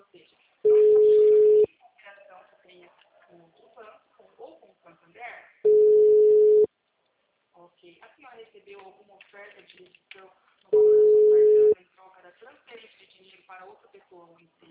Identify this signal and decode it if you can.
Ringback tone (Europe, 425 Hz), with speech.